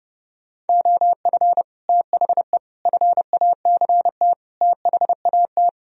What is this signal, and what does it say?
Morse code, heard as a keyed tone, about 30 wpm, 700 Hz. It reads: OF THE FACT THAT